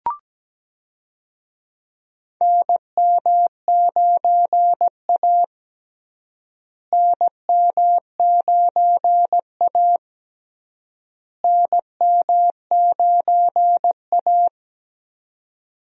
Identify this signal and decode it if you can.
Morse code, heard as a keyed tone, about 17 wpm, 700 Hz. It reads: NM9A NM9A NM9A
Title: USA Call Signs - 4 Characters - Repeated 3x - 17wpm